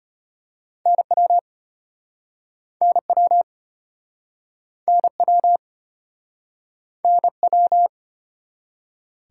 Morse code, heard as a keyed tone, about 30 wpm, 700 Hz. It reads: NW NW NW NW
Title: Speed-Racing Top 180 QSO Elements 25wpm